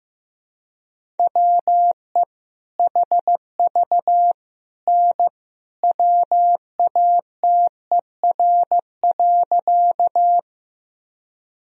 Morse code, heard as a keyed tone, about 16 wpm, 700 Hz. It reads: WE HV N WATER.
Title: Sentences from Top 100 Words and Abbreviations 15wpm